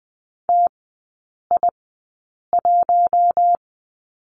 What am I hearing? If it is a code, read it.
Morse code, heard as a keyed tone, about 20 wpm, spaced as for 8 wpm, 700 Hz. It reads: TI1
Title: 047 Lesson 9e - Number 4 - Three Characters 20wpm